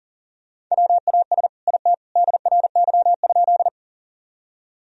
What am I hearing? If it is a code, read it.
Morse code, heard as a keyed tone, about 40 wpm, 700 Hz. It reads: WAS IT DRY?
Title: Sentences from Top 1000 Words - Review Part 3 - 40wpm